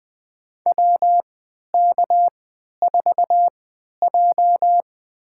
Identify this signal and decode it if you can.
Morse code, heard as a keyed tone, about 20 wpm, spaced as for 11 wpm, 700 Hz. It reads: WK4J
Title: USA Call Signs - 4 Characters - 3x Character Spacing - 20wpm